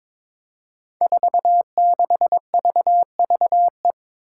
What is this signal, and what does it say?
Morse code, heard as a keyed tone, about 22 wpm, 700 Hz. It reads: 46VVE
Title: Five Challenging Letter-Number Combinations 22wpm